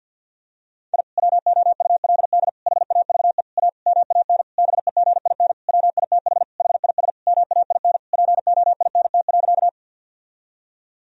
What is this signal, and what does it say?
Morse code, heard as a keyed tone, about 50 wpm, 700 Hz. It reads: I WOULD HAVE A MAN BEGIN WITH HIS MAIN POINT.
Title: Sentences from Top 1000 Words - Review Part 2 - 50wpm